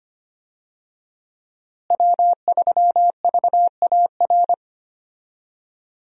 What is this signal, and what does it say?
Morse code, heard as a keyed tone, about 25 wpm, 700 Hz. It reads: W3VAR
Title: USA Call Signs - 5 Characters - Repeated 3x - 25wpm